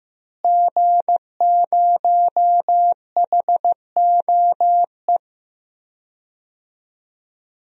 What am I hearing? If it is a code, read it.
Morse code, heard as a keyed tone, about 15 wpm, 700 Hz. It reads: G0HOE
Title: DXCC-100 Call Signs - 5 Characters - Repeated 3x - 15wpm